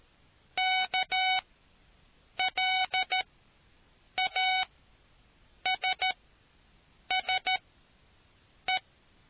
Morse code, heard as a keyed tone, about 13 wpm, spaced as for 6 wpm, 750 Hz. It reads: KLASSE